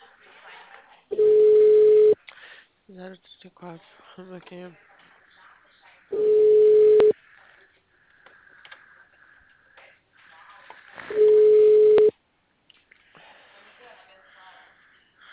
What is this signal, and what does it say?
Ringback tone (Europe, 425 Hz), with speech.
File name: out- -2015- - - .1290